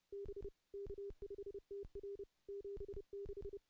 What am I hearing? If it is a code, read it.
Morse code, heard as a keyed tone, about 30 wpm, 400 Hz. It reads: B K5TR 76
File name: contests / CW-SS-k5tr- -0400